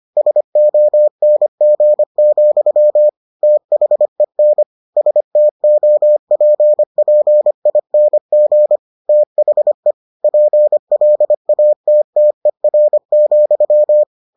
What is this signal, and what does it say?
Morse code, heard as a keyed tone, about 25 wpm, 600 Hz. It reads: SONG, THEN STOPPING THE PLATTER,